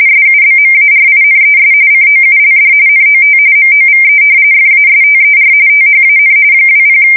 RTTY, about 45.45 baud. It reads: SUBMARINES COULD NOT MOVE FASTER THAN 20
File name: fsk170 / usnavy4